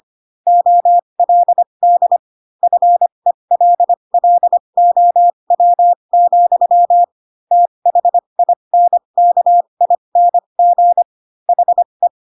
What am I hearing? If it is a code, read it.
Morse code, heard as a keyed tone, about 25 wpm, spaced as for 22 wpm, 700 Hz. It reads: OLD FELLOW, THINKING HE
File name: Pinocchio / Pinocchio ch6 18 25 ik1hge